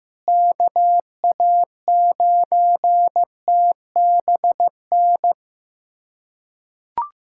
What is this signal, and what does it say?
Morse code, heard as a keyed tone, about 16 wpm, 700 Hz. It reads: KA9TBN
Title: USA Callsign DE Callsign 15wpm